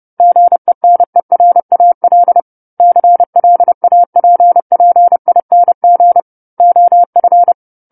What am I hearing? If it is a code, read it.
Morse code, heard as a keyed tone, about 30 wpm, 700 Hz. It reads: GENERAL CLAPPING OF
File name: 10-Chapter-10 30 WPM